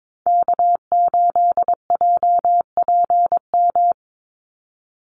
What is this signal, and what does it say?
Morse code, heard as a keyed tone, about 22 wpm, 700 Hz. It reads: K8JPM
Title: Call Signs Repeated 3x - 22wpm